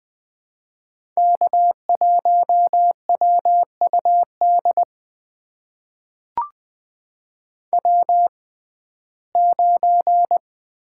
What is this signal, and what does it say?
Morse code, heard as a keyed tone, about 20 wpm, spaced as for 14 wpm, 700 Hz. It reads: K1WUD W9
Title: USA Call Signs - 6x Character Spacing - 20wpm